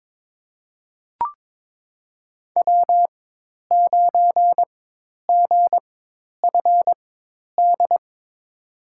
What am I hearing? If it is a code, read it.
Morse code, heard as a keyed tone, about 22 wpm, spaced as for 10 wpm, 700 Hz. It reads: W9GFD
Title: USA Call Signs - 5 Characters - 4x Character Spacing - 22wpm